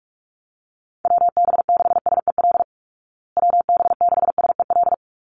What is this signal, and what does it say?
Morse code, heard as a keyed tone, about 44 wpm, 700 Hz. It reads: WB6SEL WB6SEL
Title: Call Signs Repeated 3x - 45wpm